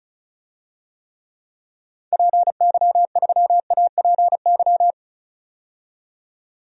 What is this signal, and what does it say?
Morse code, heard as a keyed tone, about 36 wpm, 700 Hz. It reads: PY3APY
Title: DXCC-100 Call Signs - 6 Characters - 35wpm